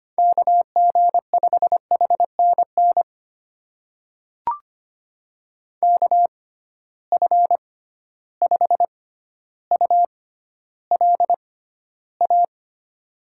Morse code, heard as a keyed tone, about 25 wpm, spaced as for 13 wpm, 700 Hz. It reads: KG5HNN KF5ULA
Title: USA Call Signs - 6x Character Spacing - 25wpm